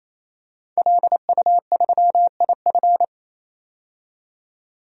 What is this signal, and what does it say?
Morse code, heard as a keyed tone, about 28 wpm, 700 Hz. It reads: LU3IF
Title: DXCC-100 Call Signs - 5 Characters - 28wpm